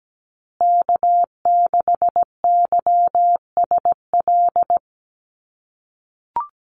Morse code, heard as a keyed tone, about 17 wpm, 700 Hz. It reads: K6YSL
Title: USA Call Signs Repeated 2x 17wpm